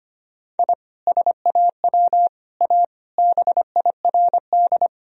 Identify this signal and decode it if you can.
Morse code, heard as a keyed tone, about 25 wpm, 700 Hz. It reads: I SAW A BIRD
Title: Sentences from Top 1000 Words - Review Part 2 - 25wpm